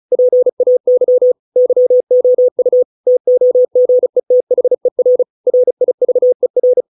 Morse code, heard as a keyed tone, about 34 wpm, 500 Hz. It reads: PAY YOU TOGETHER RIVER